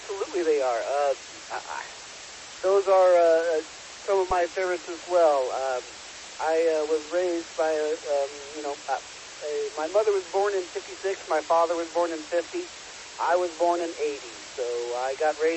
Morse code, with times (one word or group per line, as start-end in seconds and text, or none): none